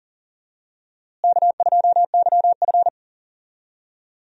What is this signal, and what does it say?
Morse code, heard as a keyed tone, about 39 wpm, 700 Hz. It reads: K2YF
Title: USA Call Signs - 4 Characters - Repeated 2x - 40wpm